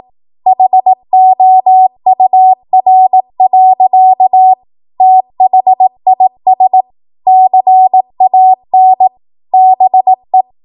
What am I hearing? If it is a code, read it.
Morse code, heard as a keyed tone, about 18 wpm, 750 Hz. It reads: HOUR. THIS CAN BE
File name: Morse / 18WPM